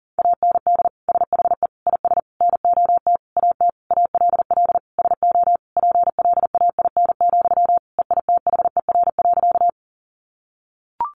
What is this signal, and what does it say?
Morse code, heard as a keyed tone, about 37 wpm, 700 Hz. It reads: AND SHE IS NOT AT ALL SO PLAIN, EITHER.
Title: Sentences from Top 600 Words 40wpm